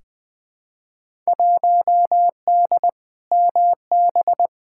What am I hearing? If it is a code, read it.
Morse code, heard as a keyed tone, about 20 wpm, 700 Hz. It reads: 1D MB
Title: Field Day Exchanges Encore - 20wpm